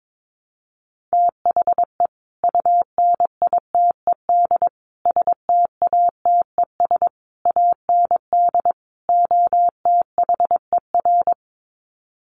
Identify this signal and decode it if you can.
Morse code, heard as a keyed tone, about 22 wpm, 700 Hz. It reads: THE UNITED STATES AND OTHER
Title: Sets of 5 Words Encore 22wpm